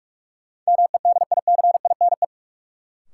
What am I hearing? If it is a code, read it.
Morse code, heard as a keyed tone, about 41 wpm, 700 Hz. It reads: MEDICINE
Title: Sentences from Top 900 Words - New Word Drill - 45wpm